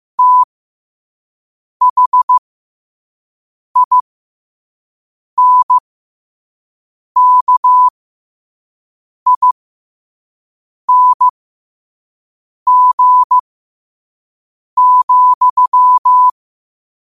Morse code, heard as a keyed tone, about 15 wpm, spaced as for 5 wpm, 1000 Hz. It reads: THINKING,
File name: qotdfiles / 5wpm